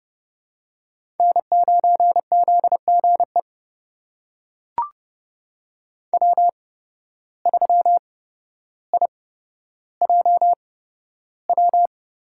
Morse code, heard as a keyed tone, about 31 wpm, spaced as for 13 wpm, 700 Hz. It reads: N9ZGE W3IJW
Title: USA Call Signs - 5 Characters - 8x Character Spacing - 30wpm